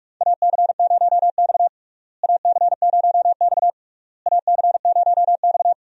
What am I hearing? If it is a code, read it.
Morse code, heard as a keyed tone, about 46 wpm, 700 Hz. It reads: AC0X AC0X AC0X
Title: USA Call Signs - 4 Characters - Repeated 3x - 45wpm